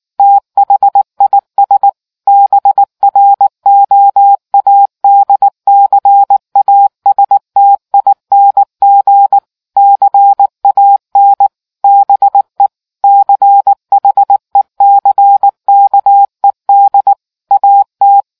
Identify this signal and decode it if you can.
Morse code, heard as a keyed tone, about 19 wpm, 800 Hz. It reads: THIS BROADCASTING CAN BE CHECKED AT